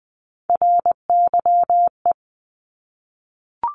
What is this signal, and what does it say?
Morse code, heard as a keyed tone, about 20 wpm, 700 Hz. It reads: RYE
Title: Multi-letter Phonemes 20wpm